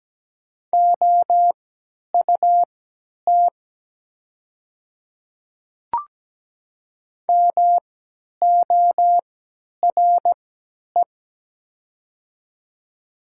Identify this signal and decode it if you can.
Morse code, heard as a keyed tone, about 17 wpm, spaced as for 10 wpm, 700 Hz. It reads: OUT MORE
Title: 3x Character Spacing - Top 100 Words - 17wpm